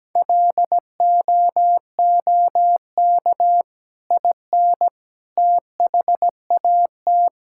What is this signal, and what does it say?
Morse code, heard as a keyed tone, about 17 wpm, 700 Hz. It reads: LOOK IN THAT
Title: Sentences from Top 1000 Words - Review Part 5 - 17wpm